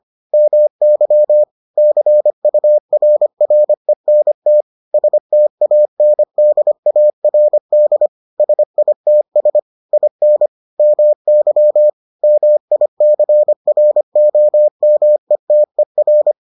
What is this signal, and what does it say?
Morse code, heard as a keyed tone, about 25 wpm, 600 Hz. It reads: MY CURRENT STANDARD SITS IN MY MICROMETER